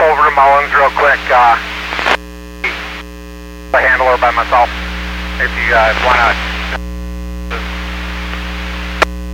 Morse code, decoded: none

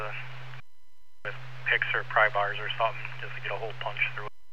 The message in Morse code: none

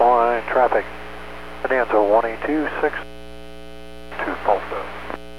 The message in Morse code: none

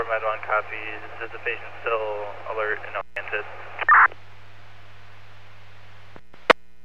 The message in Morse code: none

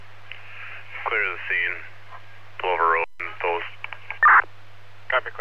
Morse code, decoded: none

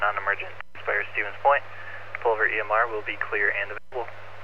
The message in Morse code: none

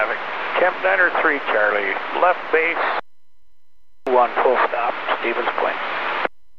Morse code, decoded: none

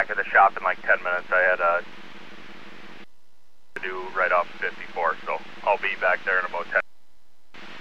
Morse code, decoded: none

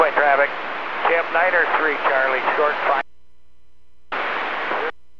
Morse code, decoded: none

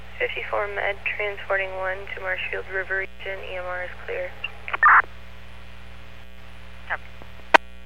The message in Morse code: none

